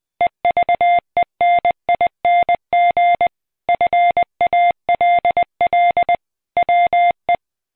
Morse code, heard as a keyed tone, about 20 wpm, 700 Hz. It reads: EVENING FALL WE